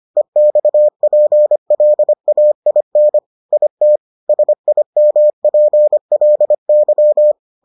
Morse code, heard as a keyed tone, about 25 wpm, 600 Hz. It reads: EXPLAIN IT SIMPLY